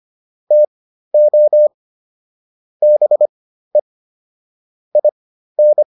Morse code, heard as a keyed tone, about 25 wpm, spaced as for 13 wpm, 600 Hz. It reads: TO BE IN